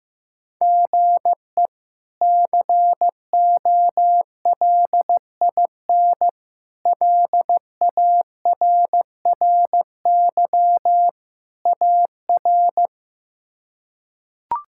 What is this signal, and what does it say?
Morse code, heard as a keyed tone, about 15 wpm, 700 Hz. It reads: GE COLIN LARRY AR
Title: K1USN SST QSO Practice - 15wpm